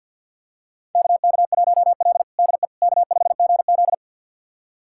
Morse code, heard as a keyed tone, about 49 wpm, 700 Hz. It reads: KK1L DE K5GZ
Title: USA Callsign DE Callsign - 4 Characters - 50wpm